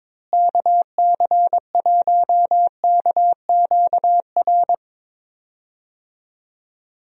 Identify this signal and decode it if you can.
Morse code, heard as a keyed tone, about 22 wpm, 700 Hz. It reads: KC1KQR
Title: USA Call Signs - 6 Characters - Repeated 2x - 22wpm